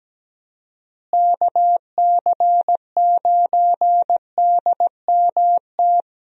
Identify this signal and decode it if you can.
Morse code, heard as a keyed tone, about 17 wpm, 700 Hz. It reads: KC9DMT
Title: USA Call Signs - 6 Characters - Repeated 2x - 17wpm